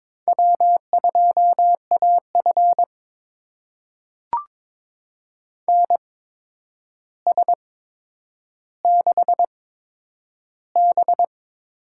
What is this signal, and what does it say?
Morse code, heard as a keyed tone, about 22 wpm, spaced as for 9 wpm, 700 Hz. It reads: W2AF NS6B
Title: USA Call Signs - 4 Characters - 8x Character Spacing - 22wpm